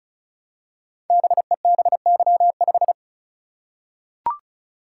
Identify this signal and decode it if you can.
Morse code, heard as a keyed tone, about 36 wpm, 690 Hz. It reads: BEBY5
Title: Code Groups - 5 Letter:Numbers - 35wpm